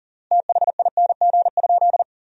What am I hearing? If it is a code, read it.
Morse code, heard as a keyed tone, about 40 wpm, 700 Hz. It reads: THING?
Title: 3x Word-Spacing - Sentences from Top 200 Words - 40wpm